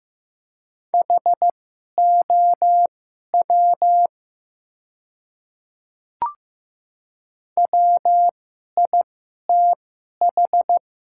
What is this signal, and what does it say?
Morse code, heard as a keyed tone, about 15 wpm, spaced as for 11 wpm, 700 Hz. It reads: HOW WITH
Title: 2x Character Spacing - Top 300 Words - 15wpm